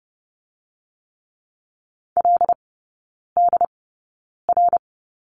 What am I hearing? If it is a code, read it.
Morse code, heard as a keyed tone, about 30 wpm, spaced as for 9 wpm, 690 Hz. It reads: LDR
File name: Lesson 17e - Letter u - Three Characters 30wpm